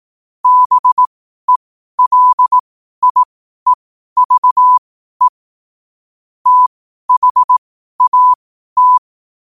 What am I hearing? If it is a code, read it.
Morse code, heard as a keyed tone, about 18 wpm, spaced as for 12 wpm, 1000 Hz. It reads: BELIEVE THAT
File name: qotdfiles / 12wpm